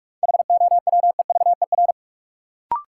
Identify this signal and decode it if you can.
Morse code, heard as a keyed tone, about 42 wpm, 700 Hz. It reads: HOWEVER
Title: Sentences from Top 600 Words - New Word Drill - 45wpm